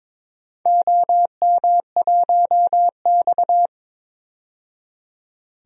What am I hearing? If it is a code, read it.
Morse code, heard as a keyed tone, about 22 wpm, 700 Hz. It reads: OM1X
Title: DXCC-100 Call Signs - 4 Characters - 22wpm